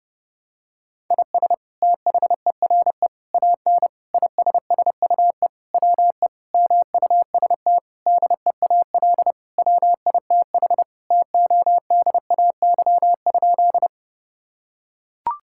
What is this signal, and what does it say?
Morse code, heard as a keyed tone, about 30 wpm, 700 Hz. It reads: IS THERE AN ISSUE WE MUST DEAL WITH TODAY?
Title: Sentences from Top 1000 Words - Review Part 6 - 30wpm